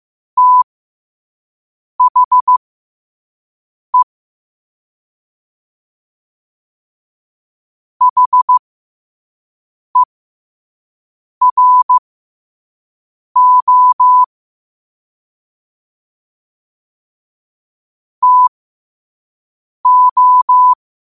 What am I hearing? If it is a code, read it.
Morse code, heard as a keyed tone, about 15 wpm, spaced as for 5 wpm, 1000 Hz. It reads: THE HERO TO
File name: qotdfiles / 5wpm